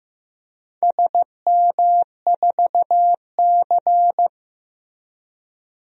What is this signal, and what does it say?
Morse code, heard as a keyed tone, about 15 wpm, 700 Hz. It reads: SM4C
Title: DXCC-100 Call Signs - 4 Characters - Repeated 2x - 15wpm